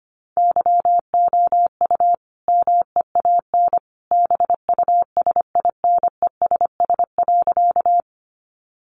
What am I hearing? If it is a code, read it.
Morse code, heard as a keyed tone, about 25 wpm, 700 Hz. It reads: YOU MEAN BUSINESS.
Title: Sentences from Top 600 Words 25wpm